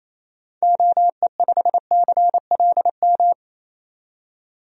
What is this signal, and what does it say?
Morse code, heard as a keyed tone, about 27 wpm, 700 Hz. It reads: OE5CLM
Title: DXCC-100 Call Signs - 6 Characters - 28wpm